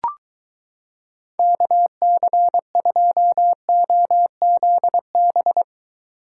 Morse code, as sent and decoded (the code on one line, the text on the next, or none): -.- -.-. ..--- --- --.. -...
KC2OZB